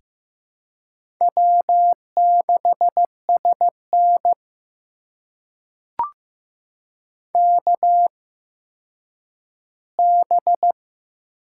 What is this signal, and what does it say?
Morse code, heard as a keyed tone, about 15 wpm, spaced as for 9 wpm, 700 Hz. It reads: W6SN KB
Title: USA Call Signs - 8x Character Spacing - 15wpm